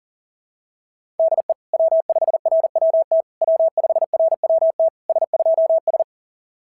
Morse code, heard as a keyed tone, about 37 wpm, 660 Hz. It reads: DE W5RWT W5RWT S2S
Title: SOTA QSO Practice 40wpm